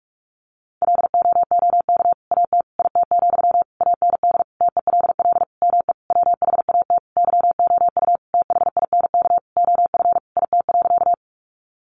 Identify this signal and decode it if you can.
Morse code, heard as a keyed tone, about 44 wpm, 700 Hz. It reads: LOOK AT IT, AND TELL ME WHAT YOU THINK OF IT.
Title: Sentences from Top 200 Words 45wpm